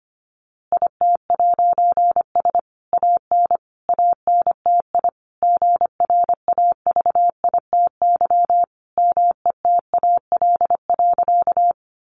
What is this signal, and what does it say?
Morse code, heard as a keyed tone, about 25 wpm, 700 Hz. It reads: IT'S AN ANTI GRAVITY METAL.
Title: A Sound Of Thunder 25wpm